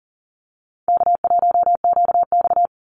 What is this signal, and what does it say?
Morse code, heard as a keyed tone, about 40 wpm, 700 Hz. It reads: K1QX
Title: Easy Call Signs Repeated 3x - 40wpm